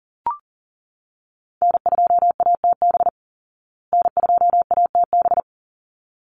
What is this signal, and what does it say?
Morse code, heard as a keyed tone, about 40 wpm, 700 Hz. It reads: N2ATB N2ATB
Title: Call Signs Repeated 3x - 40wpm